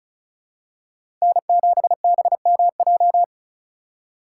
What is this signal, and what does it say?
Morse code, heard as a keyed tone, about 36 wpm, 690 Hz. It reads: N7BMJ
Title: USA Call Signs - 5 Characters - 3x Character Spacing - 35wpm